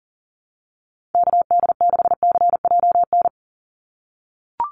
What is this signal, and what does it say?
Morse code, heard as a keyed tone, about 40 wpm, 700 Hz. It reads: KD6CJN
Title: USA Call Signs 40wpm